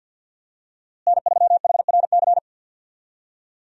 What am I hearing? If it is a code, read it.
Morse code, heard as a keyed tone, about 47 wpm, 700 Hz. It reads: N3HRC
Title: USA Call Signs - 5 Characters - Repeated 3x - 50wpm